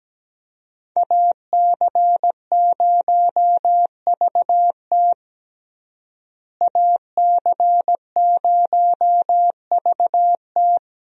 Morse code, heard as a keyed tone, about 17 wpm, 700 Hz. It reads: AC0VT AC0VT
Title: USA Call Signs - 5 Characters - Repeated 2x - 17wpm